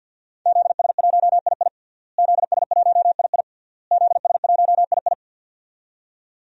Morse code, heard as a keyed tone, about 52 wpm, 700 Hz. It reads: ZS1II ZS1II ZS1II
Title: DXCC-100 Call Signs - 5 Characters - Repeated 3x - 50wpm